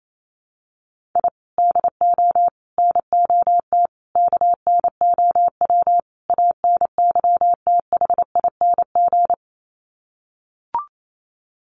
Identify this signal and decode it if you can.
Morse code, heard as a keyed tone, about 28 wpm, 700 Hz. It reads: I DO NOT KNOW ANYTHING
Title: Sets of 5 Words Encore 28wpm